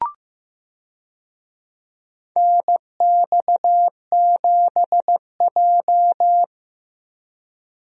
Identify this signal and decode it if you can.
Morse code, heard as a keyed tone, about 15 wpm, 700 Hz. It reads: NX7J
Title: USA Call Signs - 4 Characters - 15wpm